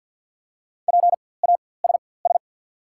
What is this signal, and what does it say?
Morse code, heard as a keyed tone, about 48 wpm, spaced as for 23 wpm, 690 Hz. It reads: PASS
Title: 4x Character Spacing - Top 500 Words - 50wpm